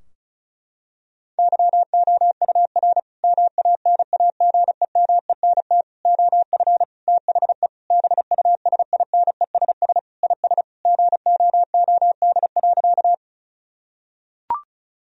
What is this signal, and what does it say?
Morse code, heard as a keyed tone, about 36 wpm, 700 Hz. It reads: YOUR MANAGEMENT OF THE BUSINESS IS GOOD.
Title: Sentences from Top 1000 Words - Review Part 4 - 35wpm